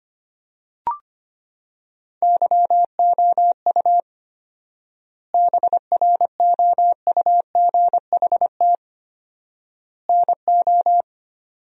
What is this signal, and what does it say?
Morse code, heard as a keyed tone, about 25 wpm, 700 Hz. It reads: YOU BROUGHT NO
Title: 4x Word-Spacing - Sentences from Top 500 Words - 25wpm